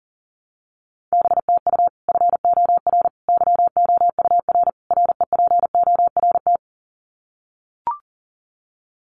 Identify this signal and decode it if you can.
Morse code, heard as a keyed tone, about 40 wpm, 700 Hz. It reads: BTU FOR YOUR REPORT
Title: Common QSO Phrases 40wpm